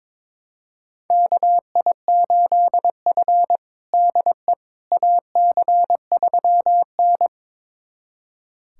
Morse code, heard as a keyed tone, about 21 wpm, 700 Hz. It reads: KI8F DE AC3N
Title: USA Callsign DE Callsign - 4 Characters - 22wpm